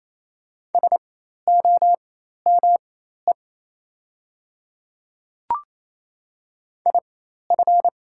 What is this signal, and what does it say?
Morse code, heard as a keyed tone, about 28 wpm, spaced as for 13 wpm, 700 Hz. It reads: SOME IF
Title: 4x Character Spacing - Top 100 Words - 28wpm